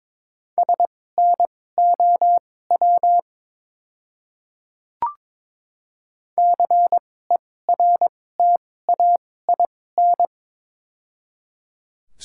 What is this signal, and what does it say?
Morse code, heard as a keyed tone, about 22 wpm, spaced as for 16 wpm, 700 Hz. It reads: SNOW CERTAIN
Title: 2x Character Spacing - Top 500 Words - 22wpm